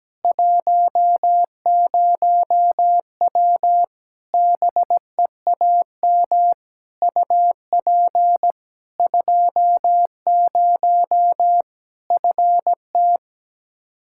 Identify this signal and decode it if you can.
Morse code, heard as a keyed tone, about 17 wpm, 700 Hz. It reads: 10W BEAM UP 20 FT
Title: Practice QSO - Exchange 2 - Info - 17wpm